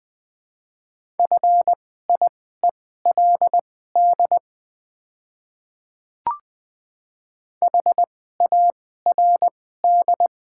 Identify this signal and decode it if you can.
Morse code, heard as a keyed tone, about 20 wpm, spaced as for 14 wpm, 700 Hz. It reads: FIELD HARD
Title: 2x Character Spacing - Top 500 Words - 20wpm